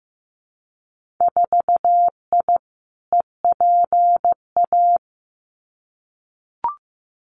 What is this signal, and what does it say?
Morse code, heard as a keyed tone, about 15 wpm, 700 Hz. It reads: 4I EPA
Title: Winter Field Day Exchanges Encore 15wpm